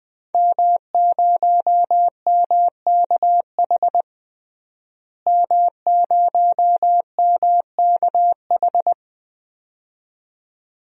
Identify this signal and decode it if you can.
Morse code, heard as a keyed tone, about 21 wpm, 700 Hz. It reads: M0MKH M0MKH
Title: DXCC-100 Call Signs - 5 Characters - Repeated 2x - 20wpm